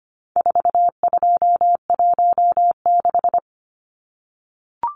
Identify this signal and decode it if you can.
Morse code, heard as a keyed tone, about 25 wpm, 700 Hz. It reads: 4216
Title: Numbers - 4-Digit 25wpm